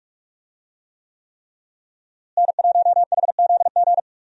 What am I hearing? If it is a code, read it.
Morse code, heard as a keyed tone, about 46 wpm, 700 Hz. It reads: N1HZG
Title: USA Call Signs - 5 Characters - Repeated 3x - 45wpm